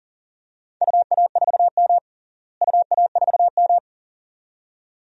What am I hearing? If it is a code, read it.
Morse code, heard as a keyed tone, about 40 wpm, 700 Hz. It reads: UA4M UA4M
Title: DXCC-100 Call Signs - 4 Characters - Repeated 2x - 40wpm